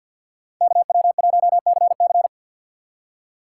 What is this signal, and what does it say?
Morse code, heard as a keyed tone, about 58 wpm, 700 Hz. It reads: KW1CC
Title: Speed Racing - USA 2x2 Call Signs 50wpm